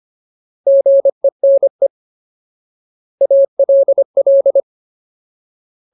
Morse code, heard as a keyed tone, about 25 wpm, 550 Hz. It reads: GENE ALL
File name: Ultimate Word List / 25WPM Ultimate